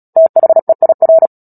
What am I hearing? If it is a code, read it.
Morse code, heard as a keyed tone, about 37 wpm, 650 Hz. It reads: THEIR
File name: Book Two / 10-Chapter 10 - Epilogue - 36 WPM